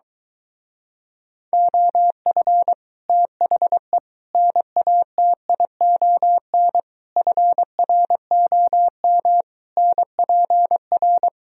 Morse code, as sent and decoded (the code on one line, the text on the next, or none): --- ..-. / - .... / -. .- - .. --- -. / ..-. .-. --- -- / -. .--. .-.
OF THE NATION FROM NPR